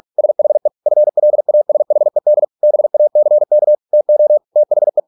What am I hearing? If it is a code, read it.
Morse code, heard as a keyed tone, about 46 wpm, 600 Hz. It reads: SHE FLASHED BACK TO THE